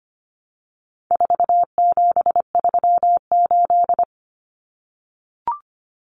Morse code, heard as a keyed tone, about 25 wpm, 700 Hz. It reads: 4738
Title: Numbers - 4-Digit 25wpm